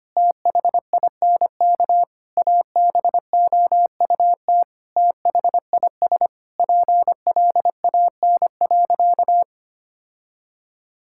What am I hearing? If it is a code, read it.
Morse code, heard as a keyed tone, about 25 wpm, 700 Hz. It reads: THINK ABOUT THIS PLAN.
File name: practice / 25wpm / Sentences with 9-Word Phrases - 25wpm